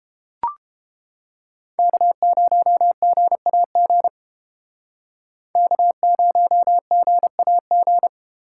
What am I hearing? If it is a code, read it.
Morse code, heard as a keyed tone, about 31 wpm, 700 Hz. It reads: K0GAG K0GAG